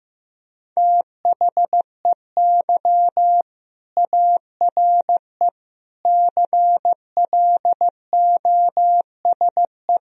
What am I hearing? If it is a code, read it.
Morse code, heard as a keyed tone, about 16 wpm, 700 Hz. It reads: THEY ARE CLOSE
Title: Sentences from Top 1000 Words - Review Part 1 - 15wpm